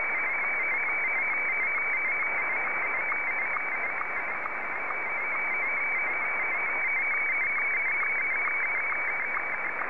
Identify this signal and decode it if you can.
Two-tone teleprinter signal, about 200 Hz shift.